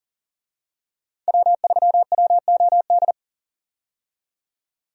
Morse code, heard as a keyed tone, about 38 wpm, 700 Hz. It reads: W3WOD
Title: USA Call Signs - 5 Characters - 40wpm